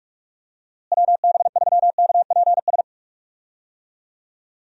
Morse code, heard as a keyed tone, about 41 wpm, 700 Hz. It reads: WB3KPS
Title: USA Call Signs - 6 Characters - Repeated 2x - 45wpm